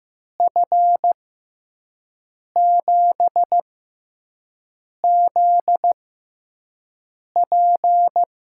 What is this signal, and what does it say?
Morse code, heard as a keyed tone, about 15 wpm, spaced as for 5 wpm, 700 Hz. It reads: F7ZP
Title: USA Call Signs - 6 Characters - 6x Character Spacing - 15wpm